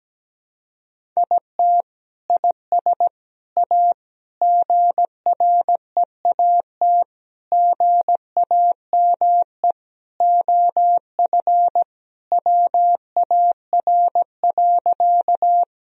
Morse code, heard as a keyed tone, about 17 wpm, 700 Hz. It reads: IT IS A GREAT GAME OF WAR.
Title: Sentences from Top 1000 Words - Review Part 3 - 17wpm